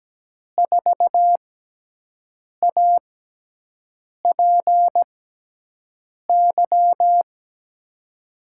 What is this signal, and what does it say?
Morse code, heard as a keyed tone, about 17 wpm, spaced as for 6 wpm, 700 Hz. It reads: 4APY
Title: USA Call Signs - 6 Characters - 6x Character Spacing - 17wpm